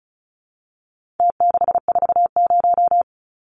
Morse code, heard as a keyed tone, about 34 wpm, 700 Hz. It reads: T640
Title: Numbers - Contest Style - 4-Digit 35wpm